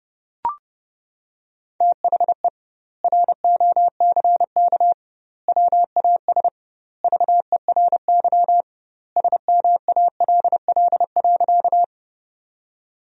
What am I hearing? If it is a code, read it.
Morse code, heard as a keyed tone, about 30 wpm, 700 Hz. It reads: THE ROCK WAS VERY SMALL.
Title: 2x Word-Spacing - Sentences from Top 400 Words - 30wpm